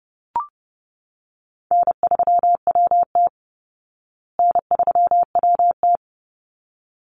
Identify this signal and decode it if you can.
Morse code, heard as a keyed tone, about 30 wpm, 700 Hz. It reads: N3WT N3WT
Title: Easy Call Signs Repeated 2x - 30wpm